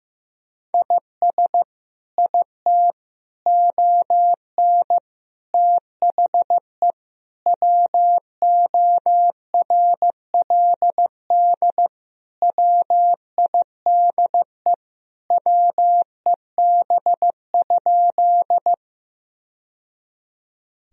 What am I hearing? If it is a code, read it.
Morse code, heard as a keyed tone, about 15 wpm, 700 Hz. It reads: IS IT ON THE WORLD WIDE WEB?
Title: Sentences from Top 1000 Words - Review Part 4 - 15wpm